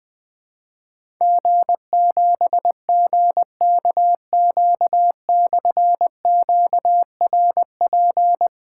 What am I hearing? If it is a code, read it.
Morse code, heard as a keyed tone, about 20 wpm, 700 Hz. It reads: G7GKQ/QRP
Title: Difficult Call Signs 20wpm